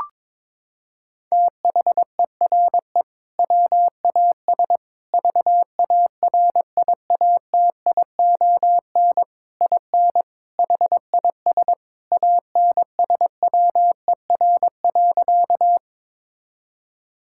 Morse code, heard as a keyed tone, about 22 wpm, 700 Hz. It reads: THERE WAS VARIATION IN HIS ANSWER.
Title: Sentences from Top 1600 Words 22wpm